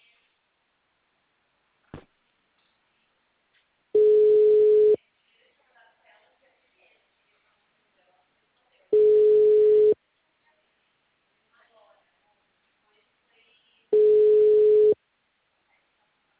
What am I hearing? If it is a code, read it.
Ringback tone (Europe, 425 Hz), with speech.